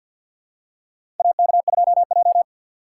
Morse code, heard as a keyed tone, about 48 wpm, 700 Hz. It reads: AK2J
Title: USA Call Signs - 4 Characters - 6x Character Spacing - 50wpm